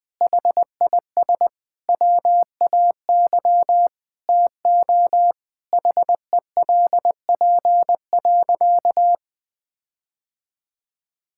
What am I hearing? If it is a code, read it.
Morse code, heard as a keyed tone, about 20 wpm, 700 Hz. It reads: HIS WAY TO HELP.
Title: Sentences with 6-Word Phrases - 20wpm